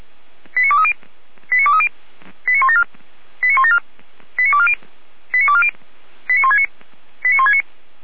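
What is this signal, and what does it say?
Five-tone selective call (ZVEI), heard as quick runs of five tones, tones of about 80 ms. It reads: 89230 89230 89174 89174 89240 89240 89169 89169
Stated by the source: ZVEI selective call received by SDR, recorded by fire brigade Orlamünde (feuerwehr-orlamuende.org).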